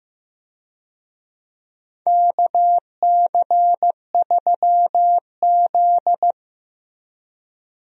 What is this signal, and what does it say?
Morse code, heard as a keyed tone, about 15 wpm, 700 Hz. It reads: KC3Z
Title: USA Call Signs - 4 Characters - Repeated 3x - 15wpm